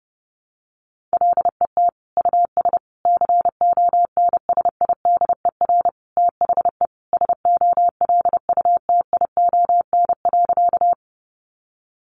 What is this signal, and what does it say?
Morse code, heard as a keyed tone, about 30 wpm, 700 Hz. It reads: LET US CONSIDER THE SOLUTION.
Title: Sentences from Top 1300 Words 30wpm